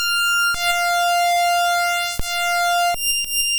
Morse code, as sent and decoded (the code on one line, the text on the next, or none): none